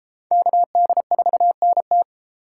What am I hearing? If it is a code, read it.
Morse code, heard as a keyed tone, about 30 wpm, 700 Hz. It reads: KD4NT